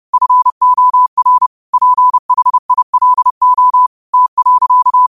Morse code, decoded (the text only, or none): FOR PHILO T.